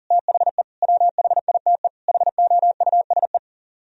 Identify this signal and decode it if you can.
Morse code, heard as a keyed tone, about 46 wpm, 700 Hz. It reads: THE WHITE HOUSE
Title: Speed-Racing - Sets of 4 Words 40wpm